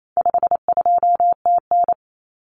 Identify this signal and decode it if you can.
Morse code, heard as a keyed tone, about 28 wpm, 700 Hz. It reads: 52TN
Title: Common Cut Numbers - 4-Digit 28wpm